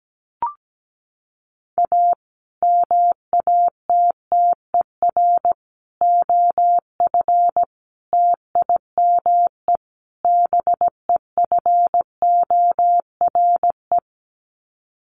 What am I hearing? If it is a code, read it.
Morse code, heard as a keyed tone, about 17 wpm, 700 Hz. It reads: A MATTER OF TIME BEFORE
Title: Sets of 5 Words Encore 17wpm